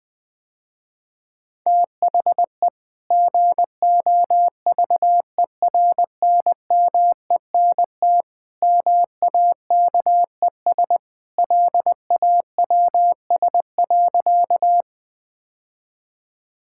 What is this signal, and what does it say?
Morse code, heard as a keyed tone, about 20 wpm, 700 Hz. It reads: THE GOVERNMENT MAKES LAWS.
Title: Sentences with 2-Word Phrases - Spoken Beginning - 20wpm